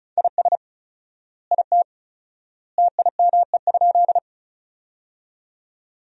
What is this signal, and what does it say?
Morse code, heard as a keyed tone, about 34 wpm, 700 Hz. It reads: IS IT TIME?
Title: 4x Word-Spacing - Sentences from Top 100 Words - 35wpm